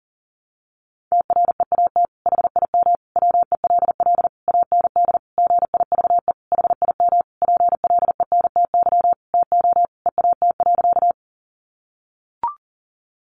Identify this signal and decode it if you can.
Morse code, heard as a keyed tone, about 40 wpm, 700 Hz. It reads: TREAT HIM WELL AND GIVE HIM PLENTY TO EAT.
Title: Sentences from Top 1300 Words 40wpm